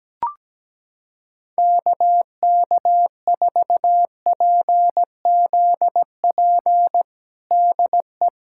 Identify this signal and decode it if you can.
Morse code, heard as a keyed tone, about 17 wpm, 700 Hz. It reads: KK4PZP DE
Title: USA Callsign DE Callsign - 6 Characters - 17wpm